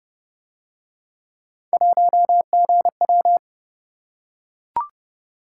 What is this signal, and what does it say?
Morse code, heard as a keyed tone, about 31 wpm, 700 Hz. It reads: 1GW